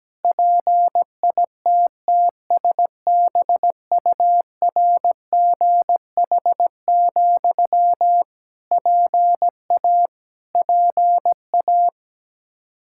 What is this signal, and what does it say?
Morse code, heard as a keyed tone, about 17 wpm, 700 Hz. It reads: PITTSBURGH, PA PA
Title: Practice QSO - Exchange 1 - Beginning - 17wpm